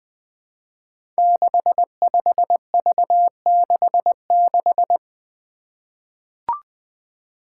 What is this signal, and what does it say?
Morse code, heard as a keyed tone, about 20 wpm, 700 Hz. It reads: 65V66
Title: Five Challenging Letter-Number Combinations 20wpm